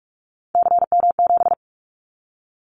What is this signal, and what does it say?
Morse code, heard as a keyed tone, about 44 wpm, 700 Hz. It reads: CM7
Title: Callsign Prefixes for Top 100 DXCC - 45wpm